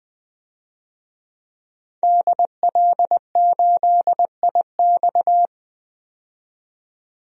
Morse code, heard as a keyed tone, about 20 wpm, 700 Hz. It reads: DL8IX
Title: DXCC-100 Call Signs - 5 Characters - 20wpm